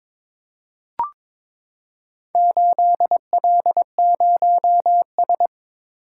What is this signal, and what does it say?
Morse code, heard as a keyed tone, about 22 wpm, 700 Hz. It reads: 8L0S
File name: practice / 22wpm / Four Letter-Number Combinations - 22wpm